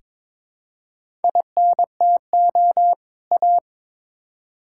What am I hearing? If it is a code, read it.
Morse code, heard as a keyed tone, about 22 wpm, 700 Hz. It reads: INTO A
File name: Build A Phrase - 2-Word Phrases - 22wpm